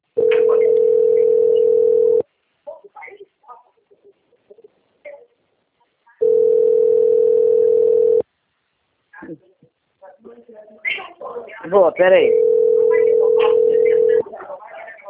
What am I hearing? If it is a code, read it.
Ringback tone (North America), 440 + 480 Hz, with speech.
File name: out- - - - - .7649